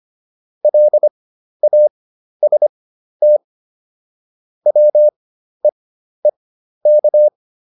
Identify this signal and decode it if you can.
Morse code, heard as a keyed tone, about 25 wpm, spaced as for 12 wpm, 600 Hz. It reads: LAST WEEK